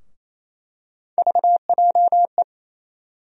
Morse code, heard as a keyed tone, about 28 wpm, 700 Hz. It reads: VJE
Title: USA Call Sign Postfixes - 3 Letters 28wpm